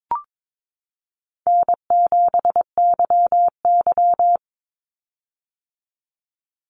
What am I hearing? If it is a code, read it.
Morse code, heard as a keyed tone, about 22 wpm, 700 Hz. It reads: N7YY
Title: Easy Call Signs - 22wpm